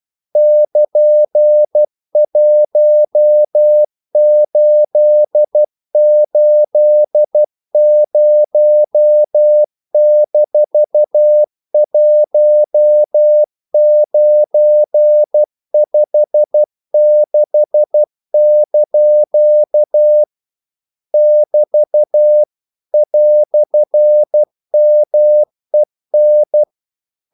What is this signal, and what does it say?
Morse code, heard as a keyed tone, about 12 wpm, 600 Hz. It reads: <KN>1880-1956) <BT> "MEN
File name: CW / Practice / EnglishQuotations0312